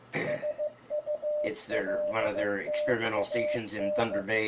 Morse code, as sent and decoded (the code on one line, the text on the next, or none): ... ..- .-.. - .- -.
SULTAN